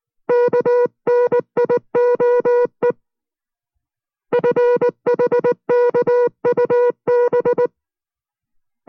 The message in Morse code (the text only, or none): KNIOE FHKUB